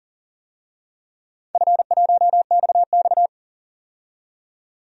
Morse code, heard as a keyed tone, about 38 wpm, 700 Hz. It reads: F1XX
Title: DXCC-100 Call Signs - 4 Characters - 40wpm